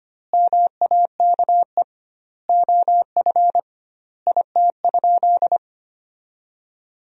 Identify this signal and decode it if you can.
Morse code, heard as a keyed tone, about 25 wpm, 700 Hz. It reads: MAKE OF IT?
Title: 2x Word-Spacing - Sentences from Top 100 Words - 25wpm